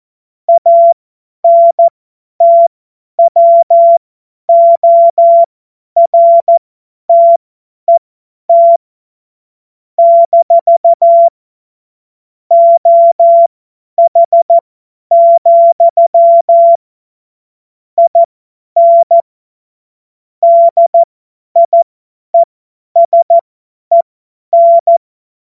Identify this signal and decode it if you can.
Morse code, heard as a keyed tone, about 14 wpm, spaced as for 10 wpm, 680 Hz. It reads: ANTWORTET - OH, IN DIESEN
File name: Morsekurs / Witze / Witz 11. Januar 2022, 50 F70